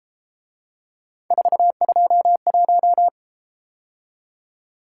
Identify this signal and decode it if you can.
Morse code, heard as a keyed tone, about 32 wpm, 700 Hz. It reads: 421